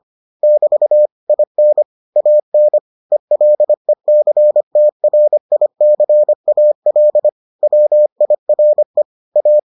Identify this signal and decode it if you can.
Morse code, heard as a keyed tone, about 25 wpm, 600 Hz. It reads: <BT> IN AN ELECTRICAL WIRE A